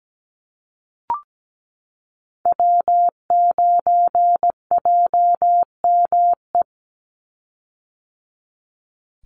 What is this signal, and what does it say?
Morse code, heard as a keyed tone, about 17 wpm, 700 Hz. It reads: W9JME
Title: USA Call Signs 17wpm